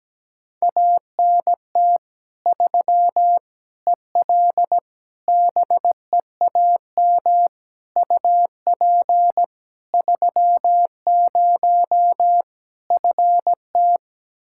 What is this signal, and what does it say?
Morse code, heard as a keyed tone, about 17 wpm, 700 Hz. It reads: ANT 3 EL BEAM UP 30 FT
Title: Practice QSO - Exchange 2 - Info - 17wpm